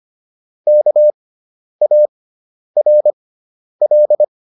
Morse code, heard as a keyed tone, about 25 wpm, spaced as for 10 wpm, 600 Hz. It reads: KARL